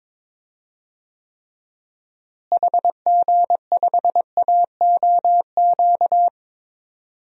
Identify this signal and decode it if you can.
Morse code, heard as a keyed tone, about 21 wpm, 700 Hz. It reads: HG5AOQ
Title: DXCC-100 Call Signs - 6 Characters - Repeated 3x - 22wpm